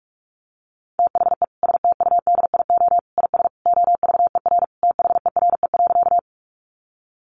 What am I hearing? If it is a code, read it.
Morse code, heard as a keyed tone, about 44 wpm, 700 Hz. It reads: THE STUDIO IS OVER THERE.
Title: Sentences from Top 1200 Words 45wpm